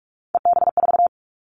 Morse code, heard as a keyed tone, about 46 wpm, 700 Hz. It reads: EB4
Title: Callsign Prefixes for Top 25 DXCC - 45wpm